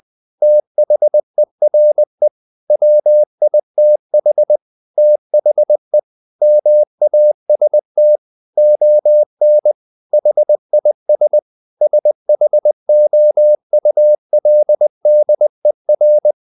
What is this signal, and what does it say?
Morse code, heard as a keyed tone, about 20 wpm, 600 Hz. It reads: THERE WITH THE MAST ON HIS SHOULDER
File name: the old man and the sea part 2 20wpm